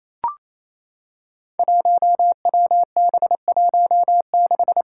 Morse code, heard as a keyed tone, about 27 wpm, 700 Hz. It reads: 1WB16
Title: Five Challenging Letter-Number Combinations 28wpm